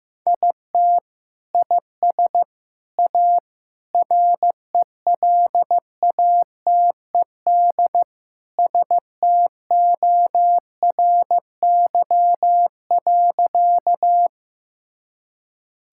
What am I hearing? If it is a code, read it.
Morse code, heard as a keyed tone, about 16 wpm, 700 Hz. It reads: IT IS A RELATED STORY.
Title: Sentences from Top 1000 Words - Review Part 5 - 15wpm